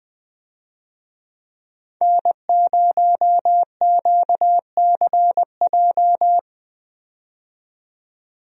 Morse code, heard as a keyed tone, about 20 wpm, 700 Hz. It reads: N0QCJ
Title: USA Call Signs - 5 Characters - 20wpm